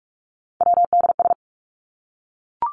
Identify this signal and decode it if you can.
Morse code, heard as a keyed tone, about 44 wpm, 700 Hz. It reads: WDS
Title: Abbreviations Repeated 3x - 45wpm